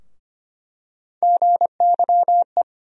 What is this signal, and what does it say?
Morse code, heard as a keyed tone, about 25 wpm, 700 Hz. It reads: GYE
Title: USA Call Sign Postfixes - 3 Letters 25wpm